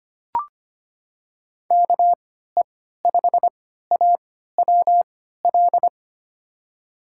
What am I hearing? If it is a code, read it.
Morse code, heard as a keyed tone, about 25 wpm, spaced as for 14 wpm, 700 Hz. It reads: KE5AWL
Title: USA Call Signs - 6 Characters - 3x Character Spacing - 25wpm